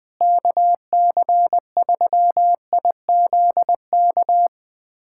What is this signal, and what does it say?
Morse code, heard as a keyed tone, about 20 wpm, 700 Hz. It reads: KC3IZK
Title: USA Callsign DE Callsign 20wpm